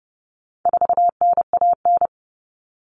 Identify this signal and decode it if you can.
Morse code, heard as a keyed tone, about 30 wpm, 700 Hz. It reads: 4NAN
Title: Common Cut Numbers - 4-Digit 30wpm